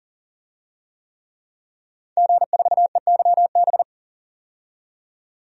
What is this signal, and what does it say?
Morse code, heard as a keyed tone, about 40 wpm, 700 Hz. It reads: G4EYB
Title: DXCC-100 Call Signs - 5 Characters - 40wpm